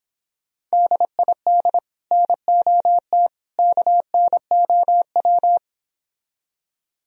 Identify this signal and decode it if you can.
Morse code, heard as a keyed tone, about 26 wpm, 700 Hz. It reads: DID NOT KNOW